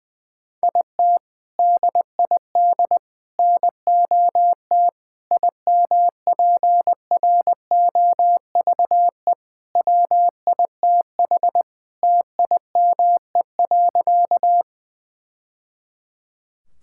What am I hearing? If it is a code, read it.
Morse code, heard as a keyed tone, about 20 wpm, 700 Hz. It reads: IT DID NOT IMPROVE WITH TIME.
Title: Sentences from Top 1000 Words - Review Part 5 - 20wpm